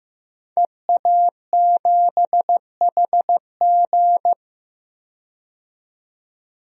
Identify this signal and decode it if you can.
Morse code, heard as a keyed tone, about 15 wpm, 700 Hz. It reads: EA7HG
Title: DXCC-100 Call Signs - 5 Characters - 15wpm